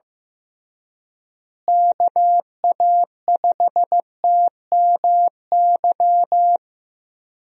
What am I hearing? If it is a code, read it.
Morse code, heard as a keyed tone, about 16 wpm, 700 Hz. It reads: KA5TMY
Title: USA Call Signs - 6 Characters - Repeated 3x - 15wpm